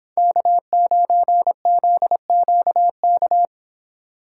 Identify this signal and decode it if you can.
Morse code, heard as a keyed tone, about 29 wpm, 700 Hz. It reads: K9ZQK